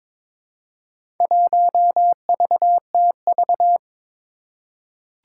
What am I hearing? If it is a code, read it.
Morse code, heard as a keyed tone, about 22 wpm, 700 Hz. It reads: 1VTV
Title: Four Challenging Letter-Number Combinations 22wpm